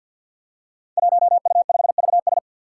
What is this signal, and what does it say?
Morse code, heard as a keyed tone, about 47 wpm, 700 Hz. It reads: 1U5VS
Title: Five Challenging Letter-Number Combinations 50wpm